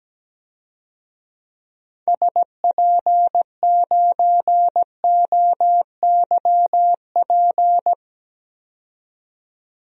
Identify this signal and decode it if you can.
Morse code, heard as a keyed tone, about 17 wpm, 700 Hz. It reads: SP9OYP
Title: DXCC-100 Call Signs - 6 Characters - 17wpm